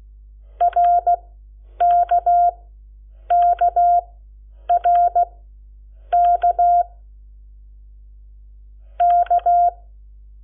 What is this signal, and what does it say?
Morse code, heard as a keyed tone, about 16 wpm, spaced as for 8 wpm, 700 Hz. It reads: RKKRK K
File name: Lecciones / CLASE Nº7